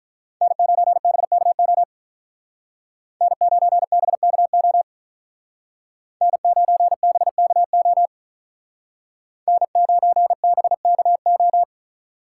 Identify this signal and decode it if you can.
Morse code, heard as a keyed tone, about 46 wpm, 700 Hz. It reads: N9BKO N9BKO N9BKO N9BKO